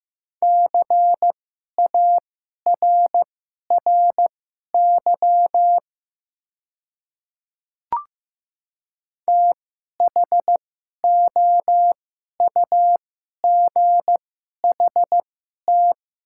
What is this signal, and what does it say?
Morse code, heard as a keyed tone, about 15 wpm, spaced as for 11 wpm, 700 Hz. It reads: CARRY THOUGHT
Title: 2x Character Spacing - Top 300 Words - 15wpm